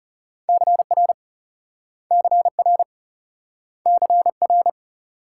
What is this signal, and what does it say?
Morse code, heard as a keyed tone, about 40 wpm, 700 Hz. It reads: CR CR CR